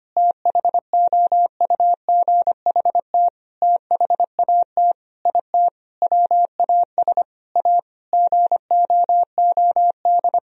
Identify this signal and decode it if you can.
Morse code, heard as a keyed tone, about 25 wpm, 700 Hz. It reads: THOUGHT THAT IT WAS A GOOD
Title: Sentences from Top 1000 Words - Review Part 4 - 25wpm